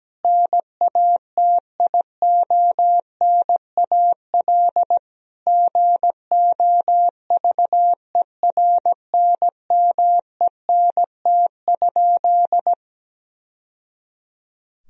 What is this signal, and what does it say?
Morse code, heard as a keyed tone, about 17 wpm, 700 Hz. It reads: NATIONAL GOVERNMENT?
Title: Sentences from Top 1000 Words - Review Part 4 - 17wpm